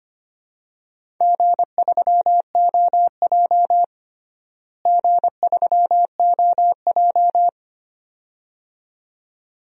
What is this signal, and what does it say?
Morse code, heard as a keyed tone, about 25 wpm, 700 Hz. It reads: G3OJ G3OJ
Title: DXCC-100 Call Signs - 4 Characters - Repeated 2x - 25wpm